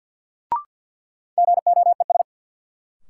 Morse code, heard as a keyed tone, about 52 wpm, 700 Hz. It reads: GOES
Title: Sentences from Top 1000 Words - Review Part 5 - New Word Drill - 50wpm